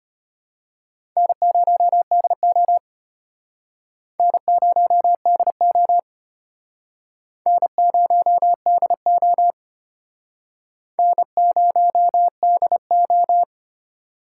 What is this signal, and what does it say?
Morse code, heard as a keyed tone, about 34 wpm, 700 Hz. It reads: N0DO N0DO N0DO N0DO